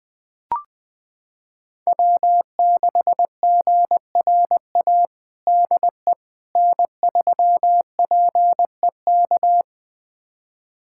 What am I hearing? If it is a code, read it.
Morse code, heard as a keyed tone, about 21 wpm, 700 Hz. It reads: W6GRA DE N3PEK
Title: USA Callsign DE Callsign 20wpm